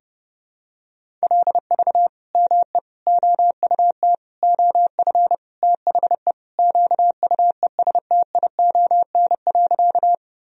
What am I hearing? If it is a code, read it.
Morse code, heard as a keyed tone, about 30 wpm, 700 Hz. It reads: LV ME OUT OF THE QUESTION.
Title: Sentences from Top 400 Words and Abbreviations 30wpm